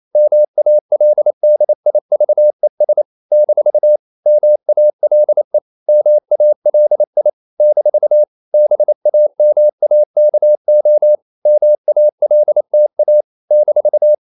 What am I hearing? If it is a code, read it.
Morse code, heard as a keyed tone, about 28 wpm, 600 Hz. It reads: MALDIVES - MALE MALI - BAMAKO MALTA -
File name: CW / Practice / NationalCapitals28